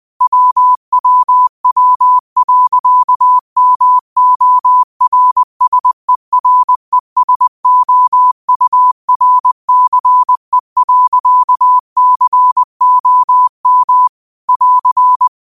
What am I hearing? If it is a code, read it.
Morse code, heard as a keyed tone, about 20 wpm, 1000 Hz. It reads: WWW.MORSERESOURCE.COM <AR>
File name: qotdfiles / 20wpm